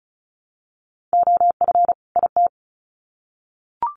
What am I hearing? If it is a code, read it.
Morse code, heard as a keyed tone, about 34 wpm, 700 Hz. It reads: OF IT
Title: Sets of 2 Words 35wpm